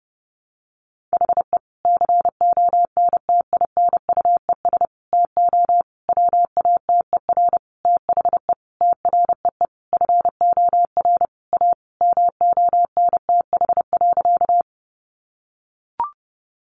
Morse code, heard as a keyed tone, about 30 wpm, 700 Hz. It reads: HE CONTINUES TO WATER THE TREE FOR A MONTH.
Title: Easy Phrases 30wpm